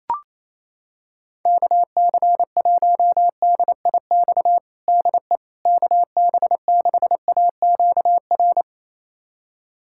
Morse code, heard as a keyed tone, about 28 wpm, 700 Hz. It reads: KC1DIX DE KB6AQR
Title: USA Callsign DE Callsign 28wpm